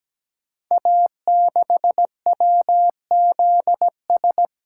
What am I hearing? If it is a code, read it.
Morse code, heard as a keyed tone, about 17 wpm, 700 Hz. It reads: A6WZS
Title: Code Groups - 5 Letter:Numbers - 17wpm